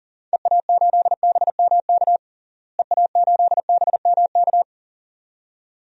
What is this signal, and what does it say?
Morse code, heard as a keyed tone, about 40 wpm, 700 Hz. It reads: EA8BMK EA8BMK
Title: DXCC-100 Call Signs - 6 Characters - Repeated 2x - 40wpm